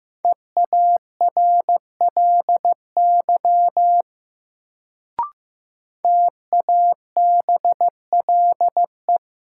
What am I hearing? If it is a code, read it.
Morse code, heard as a keyed tone, about 15 wpm, 700 Hz. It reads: EARLY TABLE
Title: Sentences from Top 400 Words - New Word Drill - 15wpm